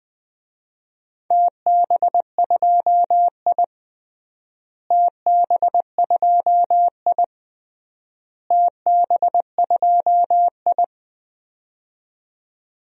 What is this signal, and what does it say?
Morse code, heard as a keyed tone, about 21 wpm, 700 Hz. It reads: TB2I TB2I TB2I
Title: DXCC-100 Call Signs - 4 Characters - Repeated 3x - 20wpm